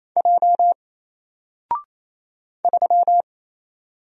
Morse code, heard as a keyed tone, about 27 wpm, 700 Hz. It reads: J 3
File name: Single-Letter-Number-28wpm